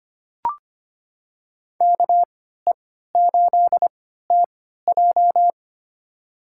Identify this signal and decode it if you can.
Morse code, heard as a keyed tone, about 25 wpm, spaced as for 14 wpm, 700 Hz. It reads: KE8TJ
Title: USA Call Signs - 5 Characters - 3x Character Spacing - 25wpm